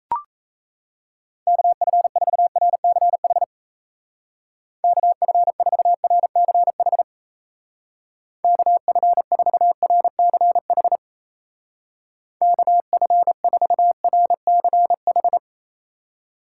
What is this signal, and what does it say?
Morse code, heard as a keyed tone, about 35 wpm, 700 Hz. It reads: KF4RCH KF4RCH KF4RCH KF4RCH
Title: Speed Racing - USA 2x3 Call Signs 28wpm